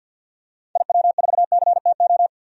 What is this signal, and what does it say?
Morse code, heard as a keyed tone, about 46 wpm, 700 Hz. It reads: IW4CTO
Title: DXCC-100 Call Signs - 6 Characters - Repeated 3x - 50wpm